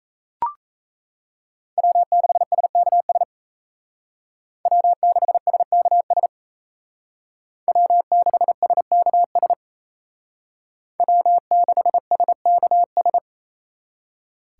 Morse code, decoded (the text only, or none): W6SKS W6SKS W6SKS W6SKS